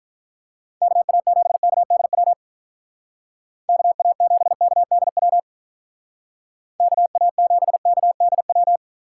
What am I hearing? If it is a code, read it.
Morse code, heard as a keyed tone, about 43 wpm, 700 Hz. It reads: KA7KDW KA7KDW KA7KDW
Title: Speed Racing - USA 2x3 Call Signs 35wpm